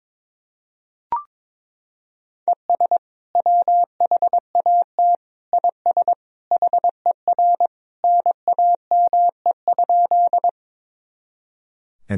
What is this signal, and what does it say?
Morse code, heard as a keyed tone, about 22 wpm, 700 Hz. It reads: ES WHAT IS HER NAME?
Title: Sentences from Top 200 Words and Abbreviations 22wpm